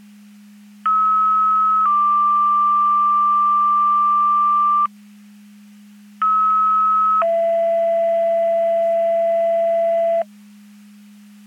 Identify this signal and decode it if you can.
Two-tone page, heard as a one-second tone, then a three-second tone, about 1290 then 1180 Hz, about 1290 then 670 Hz. It is heard from 0.9 to 10.2 s.
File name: Portage 154.385 Paging